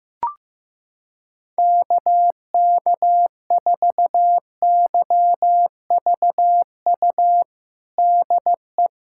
Morse code, heard as a keyed tone, about 15 wpm, 700 Hz. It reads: KK4YVU DE
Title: USA Callsign DE Callsign 15wpm